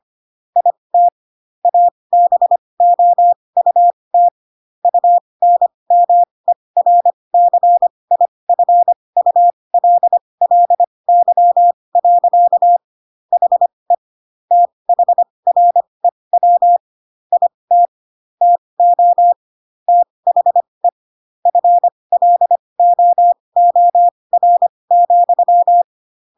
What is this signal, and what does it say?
Morse code, heard as a keyed tone, about 25 wpm, spaced as for 20 wpm, 700 Hz. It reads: IT ABOUT UNMERCIFULLY. HE THREW IT TO THE FLOOR,
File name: Pinocchio / Pinocchio ch1 15 25 ik1hge